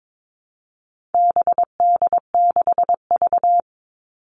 Morse code, heard as a keyed tone, about 22 wpm, 700 Hz. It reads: BD6V
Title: All Cut Numbers - 4-Digit 22wpm